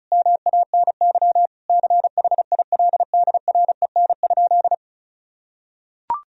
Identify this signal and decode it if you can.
Morse code, heard as a keyed tone, about 36 wpm, 700 Hz. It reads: MANY CHILDREN?
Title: Sentences from Top 300 Words and Abbreviations 35wpm